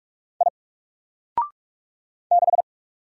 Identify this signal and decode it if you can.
Morse code, heard as a keyed tone, about 44 wpm, 700 Hz. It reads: I 6
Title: Single Letter-Number Alt 45wpm